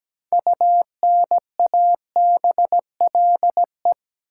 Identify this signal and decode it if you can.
Morse code, heard as a keyed tone, about 17 wpm, 700 Hz. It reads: UNABLE
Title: Six Letter Words - 17wpm